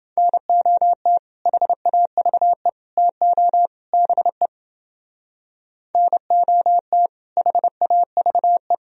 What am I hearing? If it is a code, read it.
Morse code, heard as a keyed tone, about 29 wpm, 700 Hz. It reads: NOT HAVE TO BE NOT HAVE